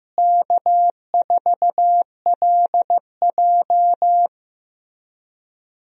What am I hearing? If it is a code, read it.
Morse code, heard as a keyed tone, about 15 wpm, 700 Hz. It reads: K4LJ
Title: USA Call Signs - 4 Characters - Repeated 3x - 15wpm